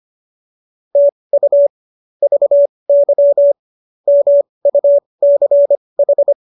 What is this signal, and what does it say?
Morse code, heard as a keyed tone, about 25 wpm, spaced as for 20 wpm, 570 Hz. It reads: TU VY MUCH